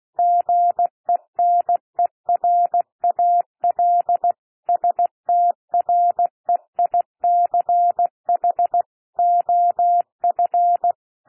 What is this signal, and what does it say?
Morse code, heard as a keyed tone, about 16 wpm, 700 Hz. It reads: GENERAL STREICH OF
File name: cwmp3 / P1 SOUND PASSAGE 16 wpm